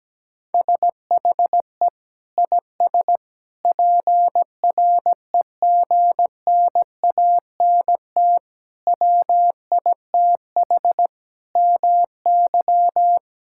Morse code, heard as a keyed tone, about 17 wpm, 700 Hz. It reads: SHE IS PREGNANT WITH MY
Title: Sentences from Top 1000 Words - Review Part 6 - 17wpm